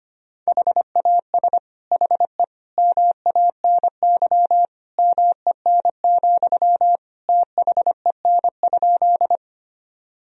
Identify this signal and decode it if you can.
Morse code, heard as a keyed tone, about 25 wpm, 700 Hz. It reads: HAS HE MANY MEN, THEN?
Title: Sentences from Top 1000 Words - Review Part 1 - 25wpm